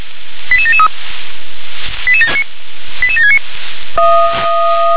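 Five-tone selective call (ZVEI), heard as quick runs of five tones, tones of about 70 ms. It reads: 88773 88769
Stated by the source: ZVEI five-tone call received on SDR.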